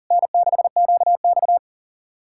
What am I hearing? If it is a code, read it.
Morse code, heard as a keyed tone, about 48 wpm, 700 Hz. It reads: N6QX
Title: Speed-Racing Common QSO Phrases Encore - 40wpm